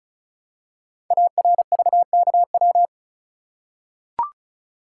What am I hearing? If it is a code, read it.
Morse code, heard as a keyed tone, about 36 wpm, 690 Hz. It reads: ARVKW
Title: Code Groups - 5 Letters - 35wpm